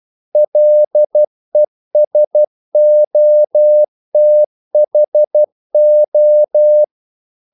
Morse code, heard as a keyed tone, about 12 wpm, 600 Hz. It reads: LESOTHO
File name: CW / Practice / NationalCapitals12